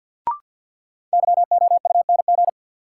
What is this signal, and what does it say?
Morse code, heard as a keyed tone, about 52 wpm, 700 Hz. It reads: YOUNG
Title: Sentences from Top 400 Words - New Word Drill - 50wpm